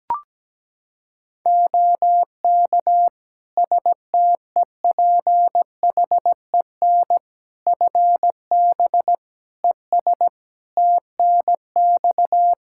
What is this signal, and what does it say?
Morse code, heard as a keyed tone, about 17 wpm, 700 Hz. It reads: OK STEPHEN FB ES TNX
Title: Practice QSO - Exchange 2 - Info - 17wpm